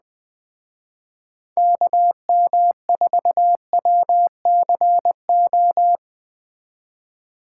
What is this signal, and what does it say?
Morse code, heard as a keyed tone, about 21 wpm, 700 Hz. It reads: KM4WCO
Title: Speed Racing - USA 2x3 Call Signs 17wpm